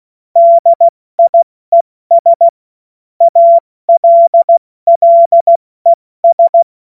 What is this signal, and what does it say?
Morse code, heard as a keyed tone, about 16 wpm, spaced as for 14 wpm, 680 Hz. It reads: DIES ALLES
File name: Morsekurs / Witze / Witz 24.Oktober 2023 70 F80